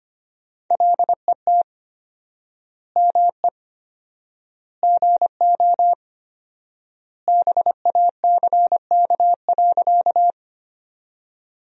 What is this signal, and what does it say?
Morse code, heard as a keyed tone, about 25 wpm, 700 Hz. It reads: LET ME GO BACK.
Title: 4x Word-Spacing - Sentences from Top 300 Words - 25wpm